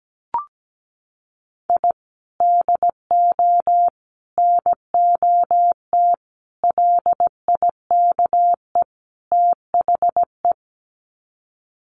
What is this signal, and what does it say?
Morse code, heard as a keyed tone, about 17 wpm, 700 Hz. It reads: I DO NOT LIKE THE
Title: Sets of 5 Words Encore 17wpm